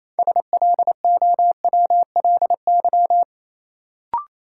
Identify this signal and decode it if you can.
Morse code, heard as a keyed tone, about 28 wpm, 700 Hz. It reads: SLOWLY
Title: Sentences from Top 1100 Words - New Word Drill - 28wpm